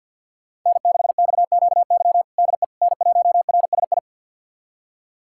Morse code, heard as a keyed tone, about 51 wpm, 700 Hz. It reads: N6XQY DE N1RSI
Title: USA Callsign DE Callsign - 5 Characters - 50wpm